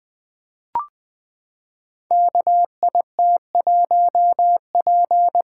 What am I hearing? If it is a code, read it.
Morse code, heard as a keyed tone, about 20 wpm, 700 Hz. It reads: KIT1P
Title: Code Groups - 5 Letter:Numbers - 20wpm